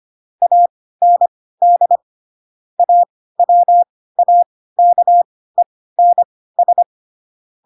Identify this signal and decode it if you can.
Morse code, heard as a keyed tone, about 25 wpm, spaced as for 16 wpm, 700 Hz. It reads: AND AWAKENS